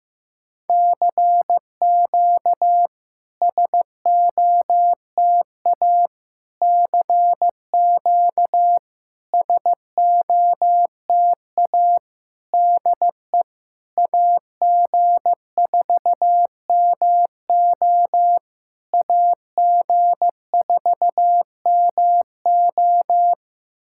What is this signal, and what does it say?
Morse code, heard as a keyed tone, about 14 wpm, 700 Hz. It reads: CQ SOTA CQ SOTA DE AG4MO AG4MO
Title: SOTA QSO Practice 15wpm